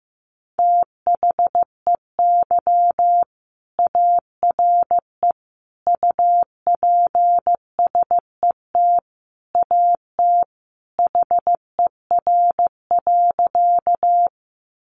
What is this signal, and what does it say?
Morse code, heard as a keyed tone, about 15 wpm, 700 Hz. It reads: THEY ARE UPSET AT HER.
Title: Sentences from Top 800 Words 15wpm